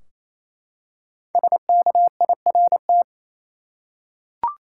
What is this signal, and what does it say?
Morse code, heard as a keyed tone, about 28 wpm, 700 Hz. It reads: SKIRT
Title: Five Letter Words - 28wpm